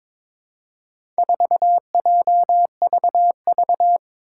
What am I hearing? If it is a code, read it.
Morse code, heard as a keyed tone, about 22 wpm, 700 Hz. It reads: 4JVV
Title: Four Challenging Letter-Number Combinations 22wpm